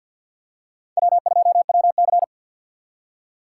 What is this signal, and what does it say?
Morse code, heard as a keyed tone, about 47 wpm, 700 Hz. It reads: W2WC
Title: USA Call Signs - 4 Characters - 3x Character Spacing - 50wpm